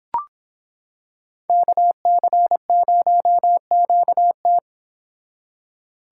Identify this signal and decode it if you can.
Morse code, heard as a keyed tone, about 25 wpm, 700 Hz. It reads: KC0QT